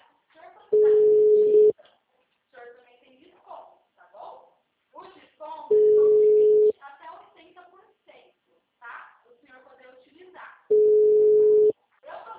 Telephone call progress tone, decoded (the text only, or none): ringback tone